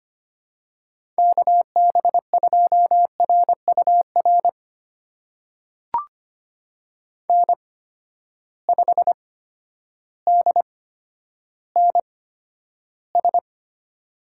Morse code, heard as a keyed tone, about 25 wpm, spaced as for 11 wpm, 700 Hz. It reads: KB2RUR N5DNS
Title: USA Call Signs - 8x Character Spacing - 25wpm